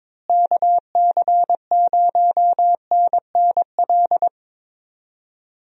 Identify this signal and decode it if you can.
Morse code, heard as a keyed tone, about 22 wpm, 700 Hz. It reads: KC0NNL
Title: USA Call Signs - 6 Characters - Repeated 3x - 22wpm